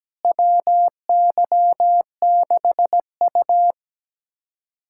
Morse code, heard as a keyed tone, about 17 wpm, 700 Hz. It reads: WY6U
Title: USA Callsign DE Callsign - 4 Characters - 17wpm